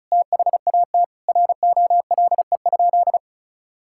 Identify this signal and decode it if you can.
Morse code, heard as a keyed tone, about 34 wpm, 700 Hz. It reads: THAT ROLE?
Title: Sentences from Top 1000 Words - Review Part 6 - 35wpm